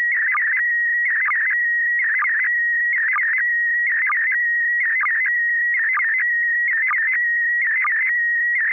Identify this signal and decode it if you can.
SSTV (slow-scan television), PD 290.
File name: lu4aao / PD290